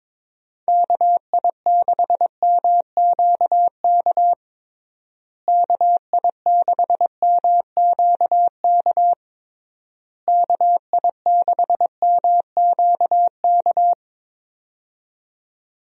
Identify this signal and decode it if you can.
Morse code, heard as a keyed tone, about 22 wpm, 700 Hz. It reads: KI6MQK KI6MQK KI6MQK
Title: USA Call Signs - 6 Characters - Repeated 3x - 22wpm